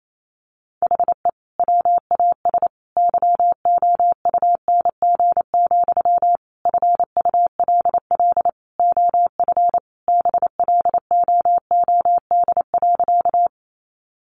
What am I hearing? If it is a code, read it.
Morse code, heard as a keyed tone, about 28 wpm, 700 Hz. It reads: HE WAS YOUNG, FULL OF BLOOD.
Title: Sentences from Top 1100 Words 28wpm